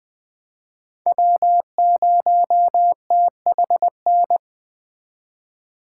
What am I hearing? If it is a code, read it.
Morse code, heard as a keyed tone, about 20 wpm, 700 Hz. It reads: W0THN
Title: Speed Racing - USA 1x3 Call Signs 17wpm